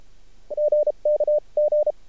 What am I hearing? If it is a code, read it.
Morse code, heard as a keyed tone, about 33 wpm, spaced as for 29 wpm, 600 Hz. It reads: GK G
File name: archived / CRSnet